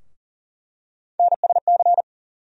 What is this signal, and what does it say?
Morse code, heard as a keyed tone, about 40 wpm, 700 Hz. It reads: NSC
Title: USA Call Sign Postfixes - 3 Letters 40wpm